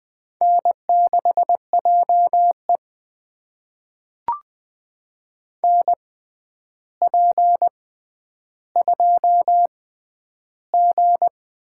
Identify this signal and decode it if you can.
Morse code, heard as a keyed tone, about 20 wpm, spaced as for 10 wpm, 700 Hz. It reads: N6JE NP2G
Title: USA Call Signs - 4 Characters - 6x Character Spacing - 20wpm